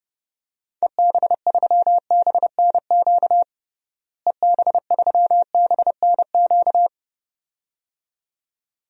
Morse code, heard as a keyed tone, about 30 wpm, 700 Hz. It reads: EB3BNQ EB3BNQ
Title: DXCC-100 Call Signs - 6 Characters - Repeated 2x - 30wpm